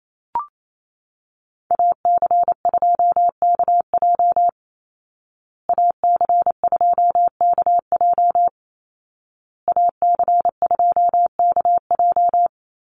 Morse code, heard as a keyed tone, about 28 wpm, 700 Hz. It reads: AC2KJ AC2KJ AC2KJ
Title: Call Signs Repeated 2x - 28wpm